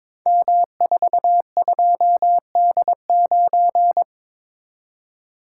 Morse code, heard as a keyed tone, about 22 wpm, 700 Hz. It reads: M42D9
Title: Code Groups - 5 Letter:Numbers - 22wpm